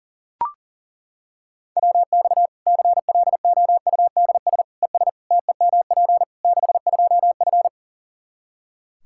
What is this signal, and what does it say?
Morse code, heard as a keyed tone, about 40 wpm, 700 Hz. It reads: WX CLOUDS ES TEMP 62F
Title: K1USN SST QSO Practice 40wpm